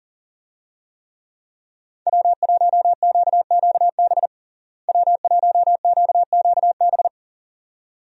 Morse code, heard as a keyed tone, about 37 wpm, 700 Hz. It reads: W1QQB W1QQB
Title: USA Call Signs - 5 Characters - Repeated 2x - 40wpm